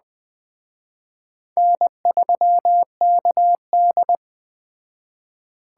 Morse code, heard as a keyed tone, about 20 wpm, 700 Hz. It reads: N3KD
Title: Speed Racing - USA 1x2 Call Signs 17wpm